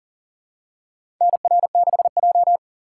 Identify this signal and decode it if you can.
Morse code, heard as a keyed tone, about 40 wpm, 700 Hz. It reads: NR6J
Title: USA Call Signs - 4 Characters - 3x Character Spacing - 40wpm